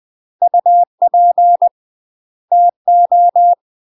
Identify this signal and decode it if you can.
Morse code, heard as a keyed tone, about 20 wpm, 700 Hz. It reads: UP TO
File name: UK MASTER QSOs TRAINING / 20WPM / UK MASTER QSOs 0060